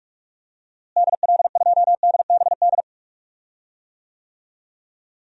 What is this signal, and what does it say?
Morse code, heard as a keyed tone, about 45 wpm, 700 Hz. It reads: DL2DBD
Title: DXCC-100 Call Signs - 6 Characters - Repeated 2x - 45wpm